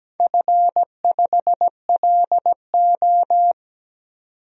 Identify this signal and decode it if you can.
Morse code, heard as a keyed tone, about 17 wpm, 700 Hz. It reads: F5LO
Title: DXCC-100 Call Signs - 4 Characters - 17wpm